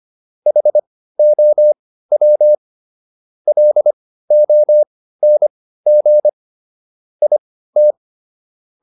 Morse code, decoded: HOW LONG IT

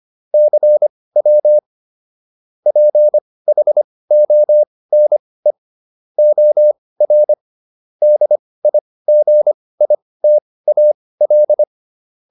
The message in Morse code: CW PHONE OR DIGITAL

